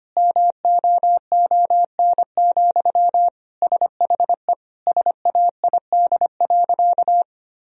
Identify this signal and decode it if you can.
Morse code, heard as a keyed tone, about 25 wpm, 700 Hz. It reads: MOON, SHE SAID.
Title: Sentences from Top 1000 Words - Review Part 3 - 25wpm